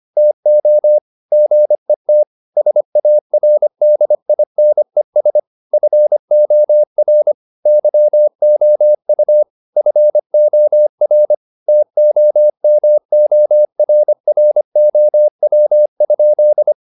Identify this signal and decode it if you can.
Morse code, heard as a keyed tone, about 25 wpm, 600 Hz. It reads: TO GET SARDINES FOR YOU FOR TOMORROW?